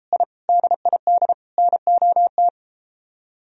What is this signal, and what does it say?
Morse code, heard as a keyed tone, about 33 wpm, 700 Hz. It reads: I DID NOT